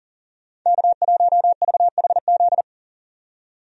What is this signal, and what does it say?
Morse code, heard as a keyed tone, about 39 wpm, 700 Hz. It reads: K1VHZ
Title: USA Call Signs - 5 Characters - 40wpm